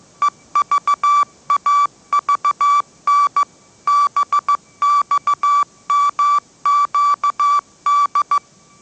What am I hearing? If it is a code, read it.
Morse code, heard as a keyed tone, about 15 wpm, 1160 Hz. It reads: EVAVN BXMQD
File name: mrx / mrx066 16wpm